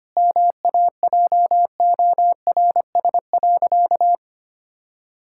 Morse code, heard as a keyed tone, about 25 wpm, 700 Hz. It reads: MAJORS.
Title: Pangram Challenge v2 - 25wpm